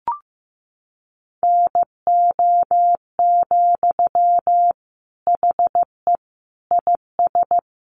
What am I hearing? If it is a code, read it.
Morse code, heard as a keyed tone, about 15 wpm, 700 Hz. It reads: NO, HE IS
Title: Sentences from Top 1400 Words 15wpm